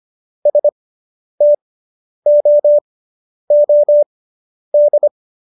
Morse code, heard as a keyed tone, about 25 wpm, spaced as for 10 wpm, 590 Hz. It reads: STOOD